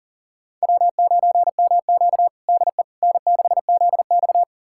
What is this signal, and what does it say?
Morse code, heard as a keyed tone, about 40 wpm, 700 Hz. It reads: W9MQ DE N6ZX
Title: USA Callsign DE Callsign - 4 Characters - 40wpm